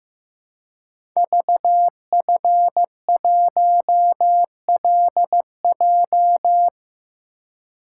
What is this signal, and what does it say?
Morse code, heard as a keyed tone, about 15 wpm, 700 Hz. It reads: VF1LJ
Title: Code Groups - 5 Letter:Numbers - 15wpm